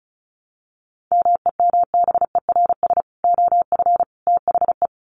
Morse code, heard as a keyed tone, about 34 wpm, 700 Hz. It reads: MEMBERS OF THE
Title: Sets of 3 Words 35wpm